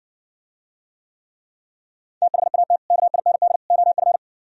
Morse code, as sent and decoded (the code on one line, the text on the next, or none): - .... .- - / -.- .. -. -.. / --- ..-.
THAT KIND OF